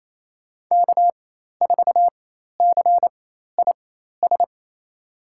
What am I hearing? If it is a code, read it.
Morse code, heard as a keyed tone, about 28 wpm, spaced as for 13 wpm, 700 Hz. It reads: K4CIS
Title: USA Call Signs - 5 Characters - 4x Character Spacing - 28wpm